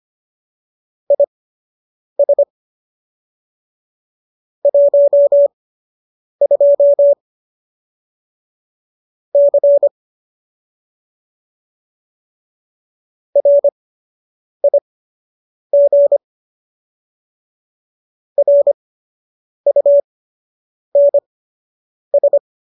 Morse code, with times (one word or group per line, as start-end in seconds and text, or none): 1.1-2.4 IS
4.7-7.1 12
9.3-9.9 C
13.4-16.2 RIG
18.4-22.4 RUNS